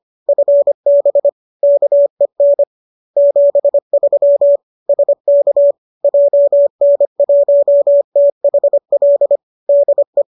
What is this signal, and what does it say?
Morse code, heard as a keyed tone, about 25 wpm, 570 Hz. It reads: FB KEN 73 SK JN1THL DE
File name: qso205 25